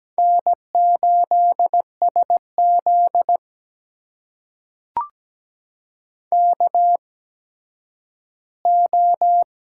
Morse code, heard as a keyed tone, about 17 wpm, spaced as for 10 wpm, 700 Hz. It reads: N8SZ KO
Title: USA Call Signs - 4 Characters - 8x Character Spacing - 17wpm